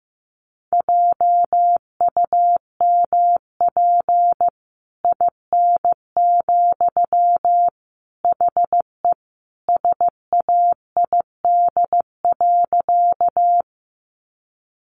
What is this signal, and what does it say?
Morse code, heard as a keyed tone, about 15 wpm, 700 Hz. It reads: JUMP IN, HE SAID.
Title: Sentences from Top 1300 Words 15wpm